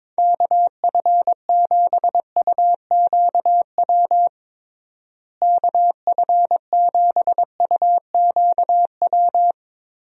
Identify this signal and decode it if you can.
Morse code, heard as a keyed tone, about 22 wpm, 700 Hz. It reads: KF7UQW KF7UQW
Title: USA Call Signs - 6 Characters - Repeated 2x - 22wpm